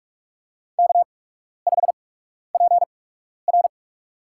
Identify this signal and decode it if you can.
Morse code, heard as a keyed tone, about 46 wpm, spaced as for 12 wpm, 700 Hz. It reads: K5PR
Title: USA Call Signs - 8x Character Spacing - 45wpm